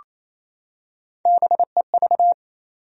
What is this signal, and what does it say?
Morse code, heard as a keyed tone, about 28 wpm, 700 Hz. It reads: BEV
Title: USA Common Names - Max 3 Letters - 28wpm